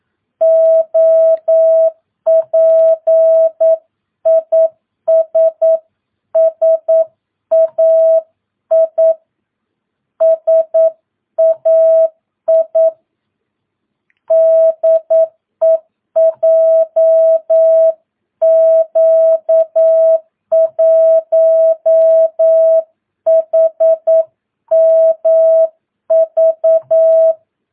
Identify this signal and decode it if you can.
Morse code, heard as a keyed tone, about 9 wpm, 650 Hz. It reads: OP IS SAI SAI DE JQ1HMV
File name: jh8cbh / 2022cw